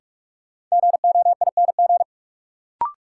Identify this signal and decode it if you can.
Morse code, heard as a keyed tone, about 45 wpm, 700 Hz. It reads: GOING
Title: Sentences from Top 500 Words - New Word Drill - 45wpm